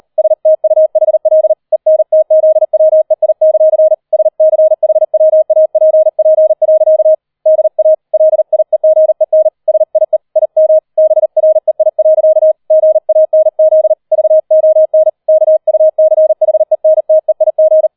Morse code, heard as a keyed tone, about 38 wpm, 630 Hz. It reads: STUHL ENTZWEI; SCHWAPP. DA LIEGEN SIE IM BREI. GANZ VON KUCHENTEIG